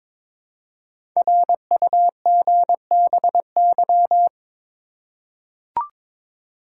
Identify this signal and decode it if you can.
Morse code, heard as a keyed tone, about 22 wpm, 700 Hz. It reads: RUGBY
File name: 5 Letter Words - First 3 Letters Given - 22wpm